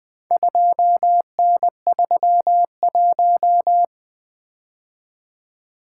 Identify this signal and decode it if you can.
Morse code, heard as a keyed tone, about 20 wpm, 700 Hz. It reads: 2N31
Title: Four Letter-Number Combinations - 20wpm